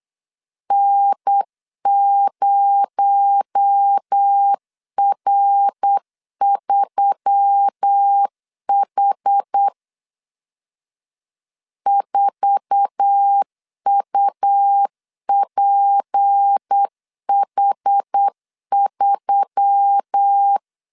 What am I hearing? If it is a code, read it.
Morse code, heard as a keyed tone, about 8 wpm, 800 Hz. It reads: N0R3H 4UPH3
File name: LEZIONI / 8 YJ45 / 8 YJ4 8wpm 3246